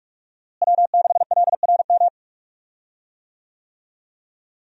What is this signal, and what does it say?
Morse code, heard as a keyed tone, about 46 wpm, 700 Hz. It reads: W6LRM
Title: USA Call Signs - 5 Characters - Repeated 3x - 45wpm